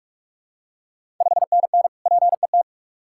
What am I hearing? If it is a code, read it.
Morse code, heard as a keyed tone, about 46 wpm, 690 Hz. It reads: 5NN PET
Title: Ontario QSO Party Exchange 45wpm